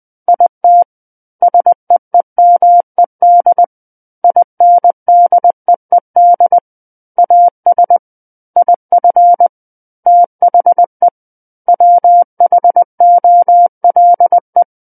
Morse code, heard as a keyed tone, about 20 wpm, 700 Hz. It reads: IT SEEMED INDEED AS IF THE WHOLE
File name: 20 WPM